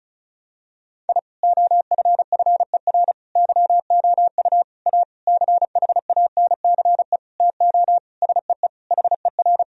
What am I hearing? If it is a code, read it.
Morse code, heard as a keyed tone, about 36 wpm, 700 Hz. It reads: I OFFER YOU A CHANCE TO SEE HER
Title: Sentences from Top 1000 Words - Review Part 4 - 35wpm